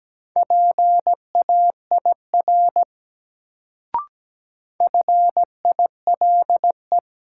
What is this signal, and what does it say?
Morse code, heard as a keyed tone, about 17 wpm, 700 Hz. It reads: PAIR FILE
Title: Sentences from Top 1200 Words - New Word Drill Alt - 17wpm